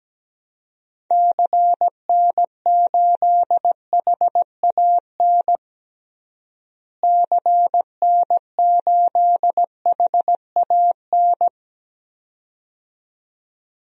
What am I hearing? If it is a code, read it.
Morse code, heard as a keyed tone, about 17 wpm, 700 Hz. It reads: CN8HAN CN8HAN
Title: DXCC-100 Call Signs - 6 Characters - Repeated 2x - 17wpm